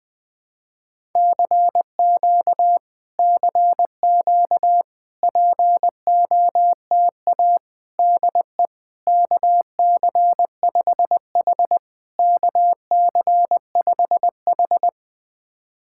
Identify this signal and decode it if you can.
Morse code, heard as a keyed tone, about 20 wpm, 690 Hz. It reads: CQ CQ POTA DE KC5H KC5H
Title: POTA QSO Practice - 20wpm